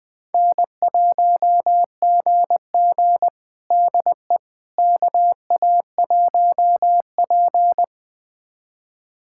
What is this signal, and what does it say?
Morse code, heard as a keyed tone, about 21 wpm, 700 Hz. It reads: N1GG DE KA1P
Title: USA Callsign DE Callsign - 4 Characters - 20wpm